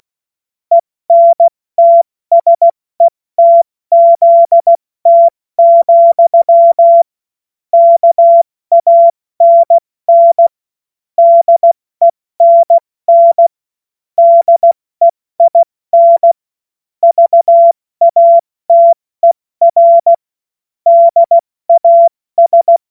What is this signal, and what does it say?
Morse code, heard as a keyed tone, about 16 wpm, 680 Hz. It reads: ENTSETZT, KANN DENN DEIN VATER DAS
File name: Witz-29-Juli-2025-70-F80